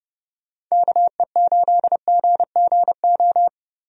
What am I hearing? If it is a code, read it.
Morse code, heard as a keyed tone, about 32 wpm, 700 Hz. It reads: KE8GGO
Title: USA Call Signs - 6 Characters - Repeated 2x - 30wpm